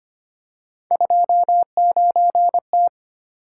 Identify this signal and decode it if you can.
Morse code, heard as a keyed tone, about 25 wpm, 700 Hz. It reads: 29T
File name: Three Letter-Number Combinations - 25wpm